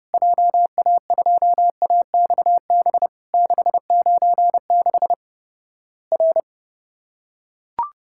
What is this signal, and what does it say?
Morse code, heard as a keyed tone, about 30 wpm, 700 Hz. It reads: JA2AXB 696 R
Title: Worked All Europe DX Contest - QTC Exchange - 30wpm